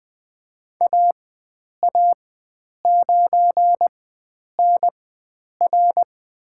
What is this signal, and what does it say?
Morse code, heard as a keyed tone, about 20 wpm, spaced as for 9 wpm, 700 Hz. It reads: AA9NR
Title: USA Call Signs - 5 Characters - 4x Character Spacing - 20wpm